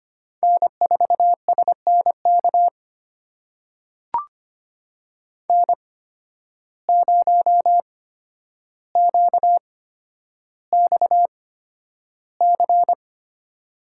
Morse code, heard as a keyed tone, about 25 wpm, spaced as for 11 wpm, 700 Hz. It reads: N4SNK N0QXC
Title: USA Call Signs - 5 Characters - 8x Character Spacing - 25wpm